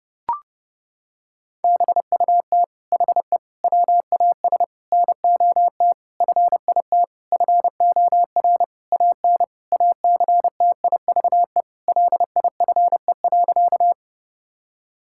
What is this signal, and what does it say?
Morse code, heard as a keyed tone, about 30 wpm, 700 Hz. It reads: BUT HE WAS NOT FIT FOR AN ACTIVE LIFE.
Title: Sentences from Top 1000 Words - Review Part 6 - 30wpm